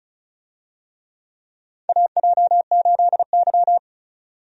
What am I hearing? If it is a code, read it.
Morse code, heard as a keyed tone, about 34 wpm, 700 Hz. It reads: AJ8Y
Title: USA Call Signs - 4 Characters - 35wpm